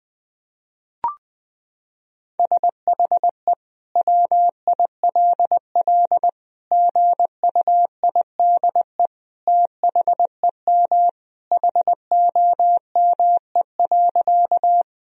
Morse code, heard as a keyed tone, about 20 wpm, 700 Hz. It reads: SHE WILL GUIDE THEM HOME.
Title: Sentences from Top 1000 Words - Review Part 3 - 20wpm